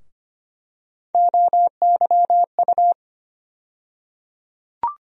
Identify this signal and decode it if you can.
Morse code, heard as a keyed tone, about 25 wpm, 700 Hz. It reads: OYU
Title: USA Call Sign Postfixes - 3 Letters 25wpm